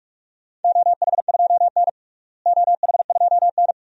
Morse code, heard as a keyed tone, about 46 wpm, 700 Hz. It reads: OH2N OH2N
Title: DXCC-100 Call Signs - 4 Characters - Repeated 2x - 45wpm